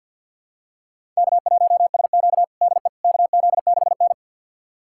Morse code, heard as a keyed tone, about 52 wpm, 700 Hz. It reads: K1SQ DE KZ6N
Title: USA Callsign DE Callsign - 4 Characters - 50wpm